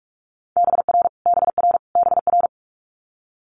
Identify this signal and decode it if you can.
Morse code, heard as a keyed tone, about 46 wpm, 700 Hz. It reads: BR BR BR
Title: Multi-letter Phonemes Repeated 3x 45wpm